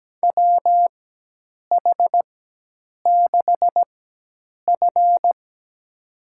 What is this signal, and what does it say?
Morse code, heard as a keyed tone, about 17 wpm, spaced as for 8 wpm, 700 Hz. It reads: WH6F
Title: USA Call Signs - 4 Characters - 4x Character Spacing - 17wpm